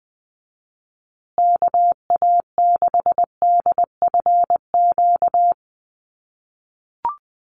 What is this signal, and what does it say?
Morse code, heard as a keyed tone, about 20 wpm, 700 Hz. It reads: KA6DFQ
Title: USA Call Signs Repeated 2x 20wpm